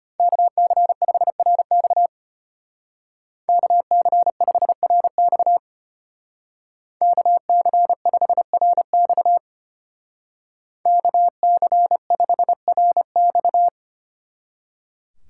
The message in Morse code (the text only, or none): KC5RX KC5RX KC5RX KC5RX